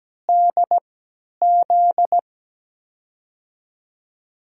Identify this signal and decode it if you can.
Morse code, heard as a keyed tone, about 17 wpm, spaced as for 10 wpm, 700 Hz. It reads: DZ
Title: USA Call Signs - 5 Characters - 3x Character Spacing - 17wpm